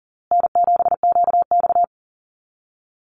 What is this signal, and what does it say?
Morse code, heard as a keyed tone, about 40 wpm, 700 Hz. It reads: N7QX
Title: USA Call Signs Repeated 3x 40wpm